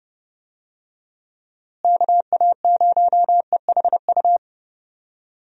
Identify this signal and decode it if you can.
Morse code, heard as a keyed tone, about 30 wpm, 700 Hz. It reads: KA0EHU
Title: USA Call Signs - 6 Characters - Repeated 3x - 30wpm